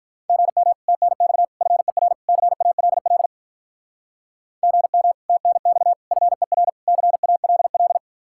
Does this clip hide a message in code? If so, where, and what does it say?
Yes: Morse code from 0.3 to 8.0 s, heard as a keyed tone, about 50 wpm, 700 Hz. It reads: GM TNX FER CALL GM TNX FER CALL